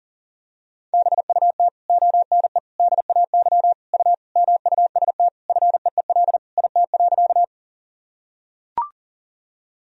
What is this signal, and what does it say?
Morse code, heard as a keyed tone, about 40 wpm, 700 Hz. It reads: BUT ONE DAY U MUST FEEL IT.
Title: Sentences from Top 400 Words and Abbreviations 40wpm